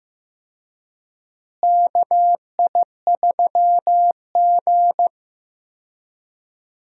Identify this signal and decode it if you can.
Morse code, heard as a keyed tone, about 16 wpm, 700 Hz. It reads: KI3G
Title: USA Call Signs - 4 Characters - Repeated 3x - 15wpm